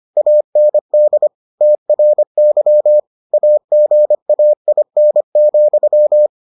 Morse code, heard as a keyed tone, about 25 wpm, 600 Hz. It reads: AND TRY AGAIN,